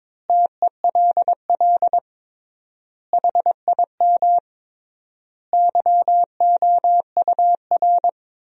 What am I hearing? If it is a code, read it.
Morse code, heard as a keyed tone, about 22 wpm, 700 Hz. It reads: TELL HIM YOUR
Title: 3x Word-Spacing - Sentences from Top 400 Words - 22wpm